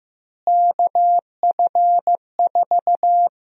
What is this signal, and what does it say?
Morse code, heard as a keyed tone, about 15 wpm, 700 Hz. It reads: KF4
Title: USA Call Sign Prefixes - Repeated 2x - 15wpm